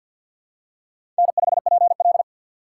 Morse code, heard as a keyed tone, about 54 wpm, 690 Hz. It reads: N5PL
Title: USA Call Signs - 4 Characters - 6x Character Spacing - 50wpm